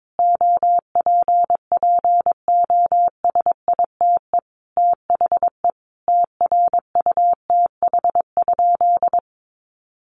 Morse code, heard as a keyed tone, about 22 wpm, 700 Hz. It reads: OPPOSITE THE TRUTH?
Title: Sentences from Top 1300 Words 22wpm